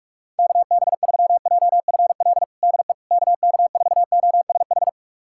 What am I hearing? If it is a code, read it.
Morse code, heard as a keyed tone, about 46 wpm, 700 Hz. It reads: KB3JFL DE KK4OSH
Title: USA Callsign DE Callsign 45wpm